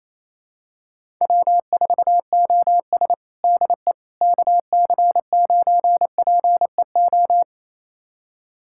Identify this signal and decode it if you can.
Morse code, heard as a keyed tone, about 28 wpm, 700 Hz. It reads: W4OS DE KC9PEO
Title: USA Callsign DE Callsign 28wpm